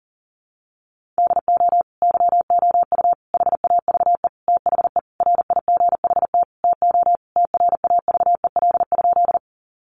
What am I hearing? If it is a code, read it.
Morse code, heard as a keyed tone, about 40 wpm, 700 Hz. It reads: DO YOU HAVE THE RIGHT TO TRAVEL?
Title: Sentences from Top 1400 Words 40wpm